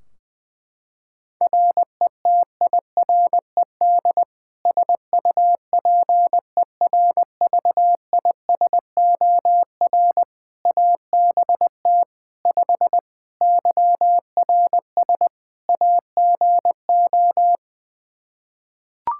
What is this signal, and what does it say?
Morse code, heard as a keyed tone, about 20 wpm, 700 Hz. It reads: RETIRED SUPERVISOR ABT 5 YRS AGO
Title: Practice QSO - Exchange 3 - More Info - 20wpm